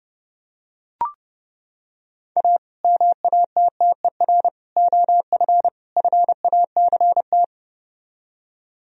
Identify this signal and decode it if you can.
Morse code, heard as a keyed tone, about 32 wpm, 700 Hz. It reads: A MATTER OF FACT